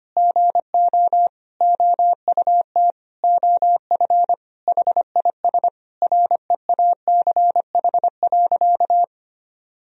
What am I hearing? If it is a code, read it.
Morse code, heard as a keyed tone, about 25 wpm, 700 Hz. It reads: GO OUT OF HIS REACH.
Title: Sentences from Top 1000 Words - Review Part 2 - 25wpm